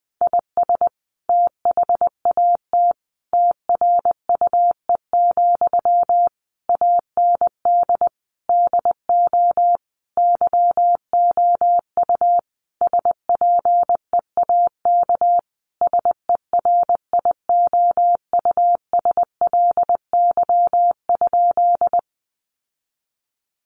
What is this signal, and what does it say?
Morse code, heard as a keyed tone, about 20 wpm, 700 Hz. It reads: IS THAT TRUE, AND DO YOU SPEAK SERIOUSLY?
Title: Sentences from Top 1200 Words 20wpm